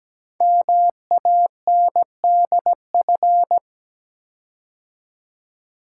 Morse code, heard as a keyed tone, about 17 wpm, 700 Hz. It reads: MANDF
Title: Code Groups - 5 Letters - 17wpm